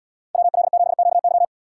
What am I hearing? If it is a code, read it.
Morse code, heard as a keyed tone, about 76 wpm, 700 Hz. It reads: WL7QY